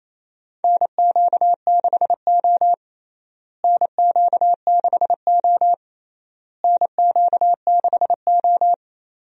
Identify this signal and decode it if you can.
Morse code, heard as a keyed tone, about 28 wpm, 700 Hz. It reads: NQ6O NQ6O NQ6O
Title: USA Call Signs - 4 Characters - Repeated 3x - 28wpm